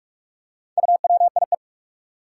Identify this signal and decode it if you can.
Morse code, heard as a keyed tone, about 43 wpm, 700 Hz. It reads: UWIE
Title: Four Challenging Letter-Number Combinations 45wpm